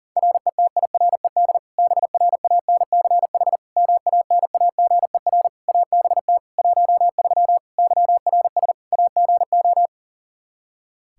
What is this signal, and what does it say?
Morse code, heard as a keyed tone, about 40 wpm, 700 Hz. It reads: RETIRED BRANCH MANAGER ABT 13 YRS AGO
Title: K1USN SST QSO Practice 40wpm